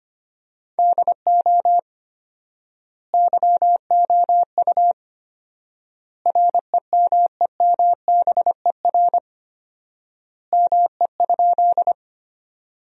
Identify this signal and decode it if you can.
Morse code, heard as a keyed tone, about 25 wpm, 700 Hz. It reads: DO YOU REMEMBER ME?
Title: 4x Word-Spacing - Sentences from Top 400 Words - 25wpm